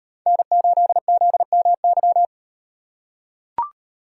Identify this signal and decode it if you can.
Morse code, heard as a keyed tone, about 38 wpm, 700 Hz. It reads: N8ZMY